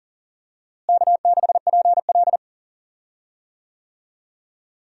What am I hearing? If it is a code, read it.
Morse code, heard as a keyed tone, about 40 wpm, 700 Hz. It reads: K6PL
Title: USA Call Signs - 4 Characters - Repeated 2x - 40wpm